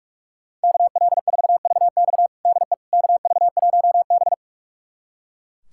Morse code, heard as a keyed tone, about 43 wpm, 700 Hz. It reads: KL4VX DE KV1B
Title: USA Callsign DE Callsign 45wpm